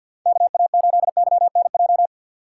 Morse code, heard as a keyed tone, about 50 wpm, 700 Hz. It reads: KA8YNJ
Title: USA Call Signs - 6 Characters - 6x Character Spacing - 50wpm